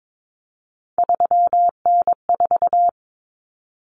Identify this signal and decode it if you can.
Morse code, heard as a keyed tone, about 22 wpm, 700 Hz. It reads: 3N4
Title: Common Cut Numbers - 3-Digit 22wpm